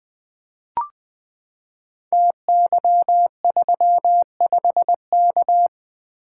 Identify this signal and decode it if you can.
Morse code, heard as a keyed tone, about 20 wpm, 700 Hz. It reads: TY35K
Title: Code Groups - 5 Letter:Numbers - 20wpm